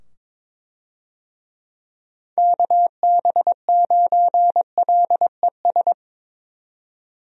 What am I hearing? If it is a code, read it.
Morse code, heard as a keyed tone, about 22 wpm, 700 Hz. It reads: KB9LES
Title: USA Call Signs - 6 Characters - 22wpm